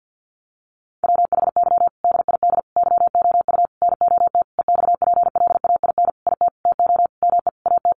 Morse code, heard as a keyed tone, about 50 wpm, 700 Hz. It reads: WHY DID YOU NOT EXPLAIN IT TO ME AT
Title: Sentences from Top 1100 Words 50wpm